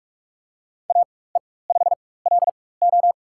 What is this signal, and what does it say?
Morse code, heard as a keyed tone, about 41 wpm, spaced as for 20 wpm, 700 Hz. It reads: AE5LO
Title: USA Call Signs - 5 Characters - 4x Character Spacing - 45wpm